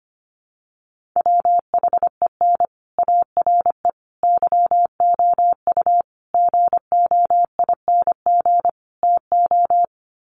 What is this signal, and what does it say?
Morse code, heard as a keyed tone, about 25 wpm, 700 Hz. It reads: WHEN ARE YOU GOING TO
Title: Sets of 5 Words Encore2 25wpm